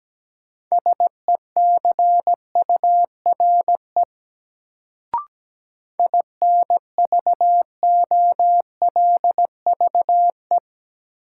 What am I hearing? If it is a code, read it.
Morse code, heard as a keyed tone, about 17 wpm, 700 Hz. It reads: SECURE INVOLVE
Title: Sentences from Top 1200 Words - New Word Drill Alt - 17wpm